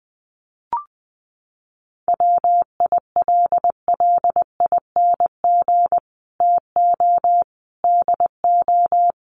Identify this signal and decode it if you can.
Morse code, heard as a keyed tone, about 20 wpm, 700 Hz. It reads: WILLING TO DO
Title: Sentences from Top 1400 Words 20wpm